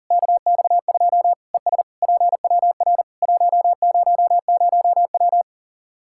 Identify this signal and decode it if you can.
Morse code, heard as a keyed tone, about 40 wpm, 700 Hz. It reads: KX2 ES PWR 100W
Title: K1USN SST QSO Practice 40wpm